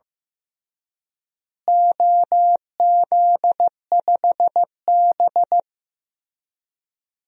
Morse code, heard as a keyed tone, about 15 wpm, 700 Hz. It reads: OZ5B
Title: DXCC-100 Call Signs - 4 Characters - Repeated 3x - 15wpm